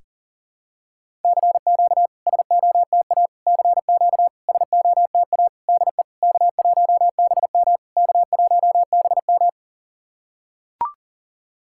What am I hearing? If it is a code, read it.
Morse code, heard as a keyed tone, about 40 wpm, 690 Hz. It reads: CQ SOTA CQ SOTA DE K1BM K1BM
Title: SOTA QSO Practice 40wpm